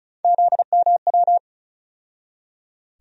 Morse code, heard as a keyed tone, about 36 wpm, 700 Hz. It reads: ZMW
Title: USA Call Sign Postfixes - 3 Letters 35wpm